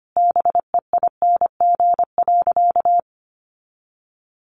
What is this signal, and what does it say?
Morse code, heard as a keyed tone, about 25 wpm, 700 Hz. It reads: BEING.
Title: Sentences from Top 1000 Words 25wpm